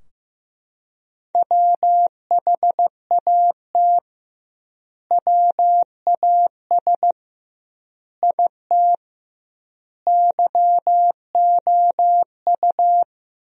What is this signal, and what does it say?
Morse code, heard as a keyed tone, about 15 wpm, 700 Hz. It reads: WHAT WAS IT YOU
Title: 2x Word-Spacing - Sentences from Top 200 Words - 15wpm